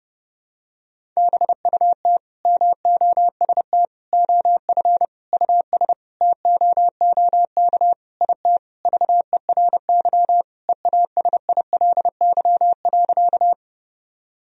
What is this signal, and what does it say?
Morse code, heard as a keyed tone, about 30 wpm, 700 Hz. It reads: BUT MOST OF US TOOK IT VERY EASILY.
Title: Sentences from Top 1000 Words - Review Part 1 - 30wpm